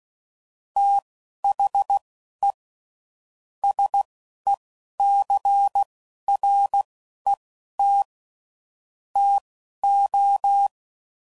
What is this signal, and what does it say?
Morse code, heard as a keyed tone, about 16 wpm, spaced as for 11 wpm, 800 Hz. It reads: THE SECRET TO
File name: CW Code Groups @10WPM / Technical Subjects / SwPwrSup